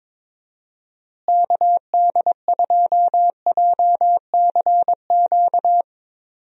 Morse code, heard as a keyed tone, about 22 wpm, 700 Hz. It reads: KD2JCQ
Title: USA Call Signs - 6 Characters - Repeated 3x - 22wpm